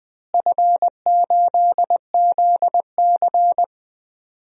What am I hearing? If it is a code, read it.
Morse code, heard as a keyed tone, about 20 wpm, 700 Hz. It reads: F8ZC
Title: DXCC-100 Call Signs - 4 Characters - Repeated 3x - 20wpm